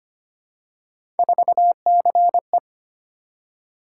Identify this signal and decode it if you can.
Morse code, heard as a keyed tone, about 25 wpm, 700 Hz. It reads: 4CE
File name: Three Letter-Number Combinations - 25wpm